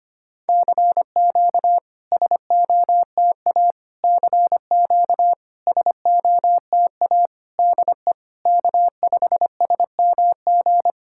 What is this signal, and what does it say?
Morse code, heard as a keyed tone, about 25 wpm, 690 Hz. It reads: CQ SOTA CQ SOTA DE K5SMG
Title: SOTA QSO Practice 25wpm